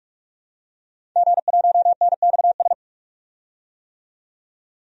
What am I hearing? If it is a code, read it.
Morse code, heard as a keyed tone, about 46 wpm, 700 Hz. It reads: G1NXS
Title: DXCC-100 Call Signs - 5 Characters - 45wpm